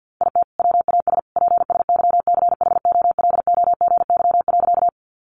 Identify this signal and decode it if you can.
Morse code, heard as a keyed tone, about 50 wpm, 700 Hz. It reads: IT WAS PSYCHOLOGY.
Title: Sentences from Top 1400 Words 50wpm